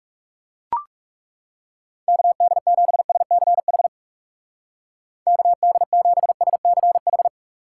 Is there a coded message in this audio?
Yes: Morse code from 2.1 to 7.3 s, heard as a keyed tone, about 38 wpm, 700 Hz. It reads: KD7SCH KD7SCH